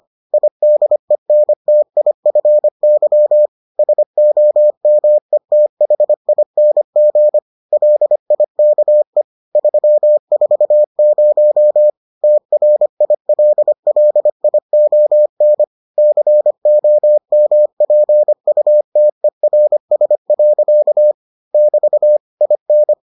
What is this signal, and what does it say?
Morse code, heard as a keyed tone, about 25 wpm, 600 Hz. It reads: IDENTIFY SOMETHING LIKE 340 TRILLION COMPUTERS. <BT> IN